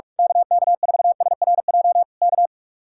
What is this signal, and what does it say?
Morse code, heard as a keyed tone, about 46 wpm, 700 Hz. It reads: KK4SRJ K
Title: Speed-Racing Common QSO Phrases Encore - 45wpm